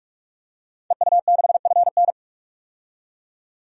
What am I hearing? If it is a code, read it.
Morse code, heard as a keyed tone, about 44 wpm, 700 Hz. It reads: EU6FN
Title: DXCC-100 Call Signs - 5 Characters - 45wpm